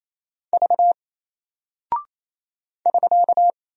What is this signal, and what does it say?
Morse code, heard as a keyed tone, about 28 wpm, 700 Hz. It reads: V <SK>
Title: Single Character (Letter:Number:Punctuation:Prosign) Alt 28wpm